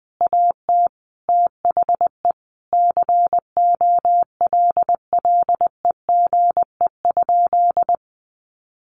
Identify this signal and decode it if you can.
Morse code, heard as a keyed tone, about 20 wpm, 700 Hz. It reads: AT THE COLLEGE?
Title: Sentences from Top 1300 Words 20wpm